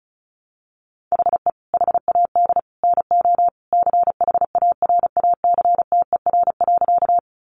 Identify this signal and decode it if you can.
Morse code, heard as a keyed tone, about 34 wpm, 700 Hz. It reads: HE HAD NO CHARACTER.
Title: Sentences from Top 1200 Words 35wpm